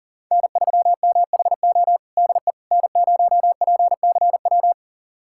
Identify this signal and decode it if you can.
Morse code, heard as a keyed tone, about 40 wpm, 700 Hz. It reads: N3MHO DE N0PCW
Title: USA Callsign DE Callsign - 5 Characters - 40wpm